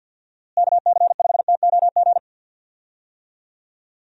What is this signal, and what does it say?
Morse code, heard as a keyed tone, about 50 wpm, 700 Hz. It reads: KC5TOG
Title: USA Call Signs - 6 Characters - Repeated 2x - 50wpm